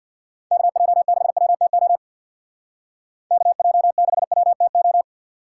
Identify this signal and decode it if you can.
Morse code, heard as a keyed tone, about 62 wpm, 700 Hz. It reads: KJ6WTO KJ6WTO
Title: Speed Racing - USA Call Signs 50wpm